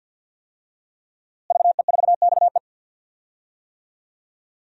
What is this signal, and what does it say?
Morse code, heard as a keyed tone, about 48 wpm, 700 Hz. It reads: VE4XE
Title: DXCC-100 Call Signs - 5 Characters - Repeated 2x - 50wpm